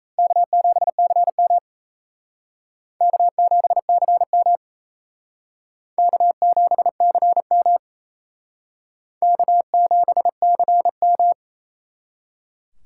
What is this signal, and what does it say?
Morse code, heard as a keyed tone, about 35 wpm, 700 Hz. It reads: K7CM K7CM K7CM K7CM
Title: Speed Racing - USA 1x2 Call Signs 28wpm